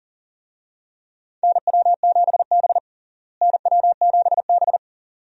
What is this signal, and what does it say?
Morse code, heard as a keyed tone, about 39 wpm, 700 Hz. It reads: NW7B NW7B
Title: USA Call Signs - 4 Characters - Repeated 2x - 40wpm